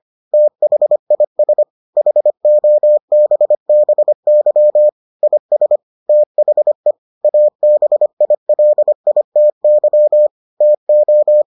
Morse code, heard as a keyed tone, about 25 wpm, 600 Hz. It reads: THIS HOBBY IS THE ABILITY TO